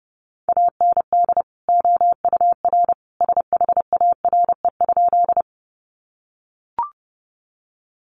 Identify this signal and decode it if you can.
Morse code, heard as a keyed tone, about 30 wpm, 700 Hz. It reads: AND OUR SHARE?
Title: Sentences from Top 900 Words 30wpm